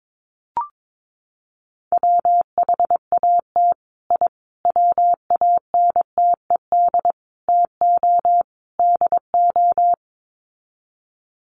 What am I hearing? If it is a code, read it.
Morse code, heard as a keyed tone, about 22 wpm, 700 Hz. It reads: WHAT I WANTED TO DO
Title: Sets of 5 Words Encore2 22wpm